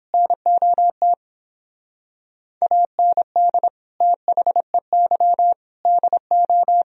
Morse code, heard as a keyed tone, about 31 wpm, 700 Hz. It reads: NOT AND THEY DO